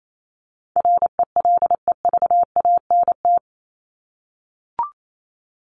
Morse code, heard as a keyed tone, about 28 wpm, 700 Hz. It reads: RELEVANT
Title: Top 1001-1250 Words 28wpm